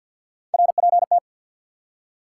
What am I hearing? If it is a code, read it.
Morse code, heard as a keyed tone, about 52 wpm, 700 Hz. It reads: RPT
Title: Top 180 QSO Elements 50wpm